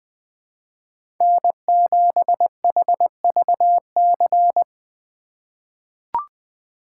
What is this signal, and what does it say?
Morse code, heard as a keyed tone, about 20 wpm, 700 Hz. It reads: N7HVC
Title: USA Call Signs - 4x Character Spacing - 20wpm